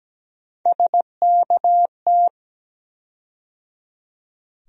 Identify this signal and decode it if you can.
Morse code, heard as a keyed tone, about 17 wpm, 700 Hz. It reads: SKT
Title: USA Call Sign Postfixes - 3 Letters 17wpm